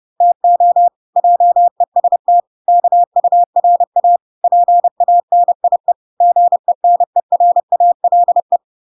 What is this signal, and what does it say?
Morse code, heard as a keyed tone, about 30 wpm, 700 Hz. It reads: TO JEST KURA PANIE GENERALE